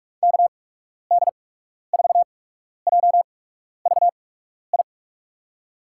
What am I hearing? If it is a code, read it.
Morse code, heard as a keyed tone, about 46 wpm, spaced as for 12 wpm, 700 Hz. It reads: KD4JVI
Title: USA Call Signs - 6 Characters - 8x Character Spacing - 45wpm